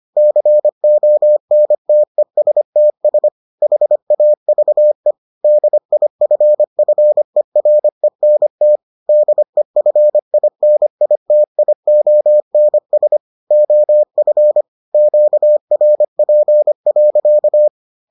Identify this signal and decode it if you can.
Morse code, heard as a keyed tone, about 25 wpm, 600 Hz. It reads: CONTESTS HAVE DIFFERENT DEFINITIONS OF QRP.